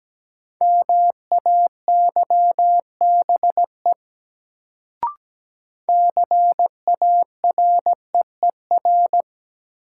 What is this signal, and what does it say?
Morse code, heard as a keyed tone, about 17 wpm, 700 Hz. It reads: MAYBE CAREER
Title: Sentences from Top 1000 Words - New Word Drill - 17wpm